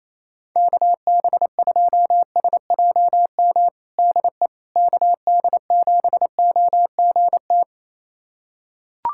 Morse code, heard as a keyed tone, about 28 wpm, 700 Hz. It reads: KB2SJM DE KD7OGT
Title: USA Callsign DE Callsign - 6 Characters - 28wpm